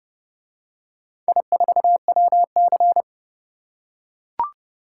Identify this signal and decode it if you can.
Morse code, heard as a keyed tone, about 30 wpm, 700 Hz. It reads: I4WC